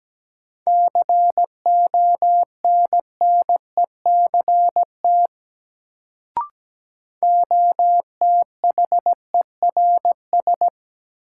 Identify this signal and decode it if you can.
Morse code, heard as a keyed tone, about 17 wpm, 700 Hz. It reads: CONNECT OTHERS
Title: Sentences from Top 1500 Words - New Word Drill Alt - 17wpm